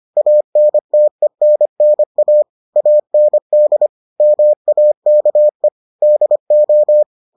Morse code, heard as a keyed tone, about 25 wpm, 600 Hz. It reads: ANTENNA AND MAKE DO